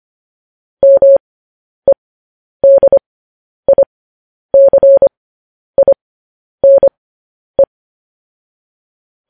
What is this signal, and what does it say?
Morse code, heard as a keyed tone, about 25 wpm, spaced as for 10 wpm, 550 Hz. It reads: MEDICINE
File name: podcast / News headlines-1x-25x10